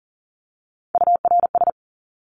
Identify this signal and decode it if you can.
Morse code, heard as a keyed tone, about 40 wpm, 700 Hz. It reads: URS
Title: Abbreviations 40wpm